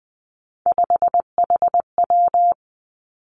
Morse code, heard as a keyed tone, about 19 wpm, 700 Hz. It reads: 5HW
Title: Three Challenging Letter-Number Combinations 20wpm